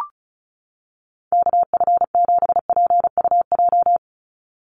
Morse code, heard as a keyed tone, about 34 wpm, 700 Hz. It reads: KF7PUJ
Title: USA Call Signs 35wpm